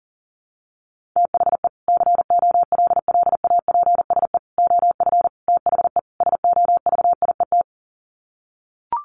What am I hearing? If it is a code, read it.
Morse code, heard as a keyed tone, about 40 wpm, 700 Hz. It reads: THE COLLAPSE OF THE SOVIET
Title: Sets of 5 Words Encore 40wpm